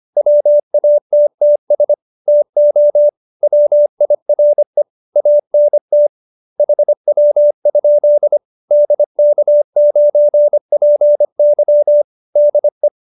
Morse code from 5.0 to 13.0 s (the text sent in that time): ANT HW? DK9PY DE